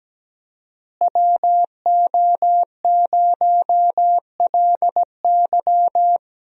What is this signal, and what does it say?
Morse code, heard as a keyed tone, about 17 wpm, 700 Hz. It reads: WO0LY
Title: USA Call Signs - 5 Characters - Repeated 3x - 17wpm